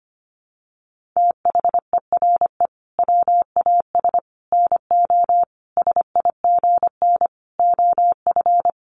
Morse code, heard as a keyed tone, about 25 wpm, 700 Hz. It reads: THERE WAS NO SIGN OF
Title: Sentences from Top 1000 Words 25wpm